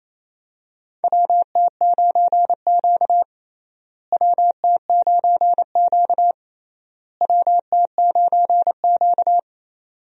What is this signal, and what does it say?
Morse code, heard as a keyed tone, about 28 wpm, 700 Hz. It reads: WT9Q WT9Q WT9Q
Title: USA Call Signs - 4 Characters - Repeated 3x - 28wpm